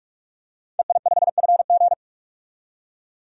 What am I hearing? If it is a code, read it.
Morse code, heard as a keyed tone, about 41 wpm, 700 Hz. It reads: EI5FG
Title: DXCC-100 Call Signs - 5 Characters - 45wpm